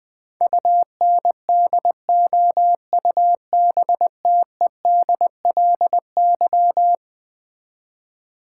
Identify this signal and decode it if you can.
Morse code, heard as a keyed tone, about 20 wpm, 700 Hz. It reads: UNDOUBTEDLY
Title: Eleven Letter Words - 20wpm